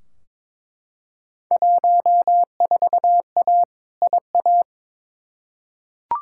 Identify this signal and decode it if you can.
Morse code, heard as a keyed tone, about 22 wpm, 700 Hz. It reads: 14A IA
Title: Field Day Exchanges Encore - 22wpm